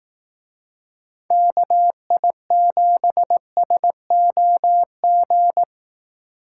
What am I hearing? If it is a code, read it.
Morse code, heard as a keyed tone, about 18 wpm, 700 Hz. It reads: KI7SOG
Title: Speed Racing - USA 2x3 Call Signs 15wpm